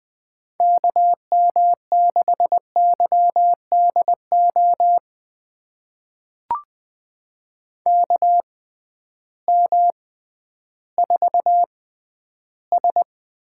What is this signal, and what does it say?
Morse code, heard as a keyed tone, about 20 wpm, spaced as for 12 wpm, 700 Hz. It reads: KM6YDO KM4S
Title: USA Call Signs - 6 Characters - 6x Character Spacing - 20wpm